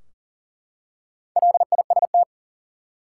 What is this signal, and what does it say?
Morse code, heard as a keyed tone, about 40 wpm, 700 Hz. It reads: LIST
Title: Four Letter Words - 40wpm